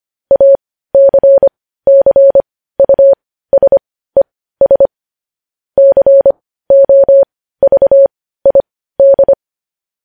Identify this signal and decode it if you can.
Morse code, heard as a keyed tone, about 25 wpm, spaced as for 15 wpm, 550 Hz. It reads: ACCUSES COVID
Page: news-headlines-in-morse-code-at-15-wpm